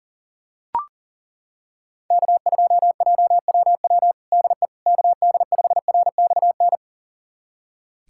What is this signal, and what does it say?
Morse code, heard as a keyed tone, about 40 wpm, 700 Hz. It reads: K2JWW DE KD5RXN
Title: USA Callsign DE Callsign 40wpm